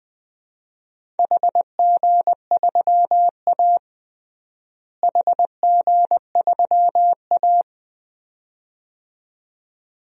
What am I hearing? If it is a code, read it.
Morse code, heard as a keyed tone, about 20 wpm, 700 Hz. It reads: HG3A HG3A
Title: DXCC-100 Call Signs - 4 Characters - Repeated 2x - 20wpm